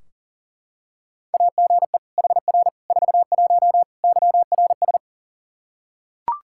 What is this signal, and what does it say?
Morse code, heard as a keyed tone, about 37 wpm, 690 Hz. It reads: AGE HR 41 YRS
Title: Practice QSO - Exchange 3 - More Info - 40wpm